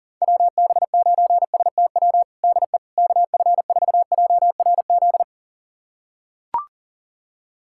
Morse code, heard as a keyed tone, about 37 wpm, 700 Hz. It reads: WB9STW DE KF4JRZ
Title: USA Callsign DE Callsign 40wpm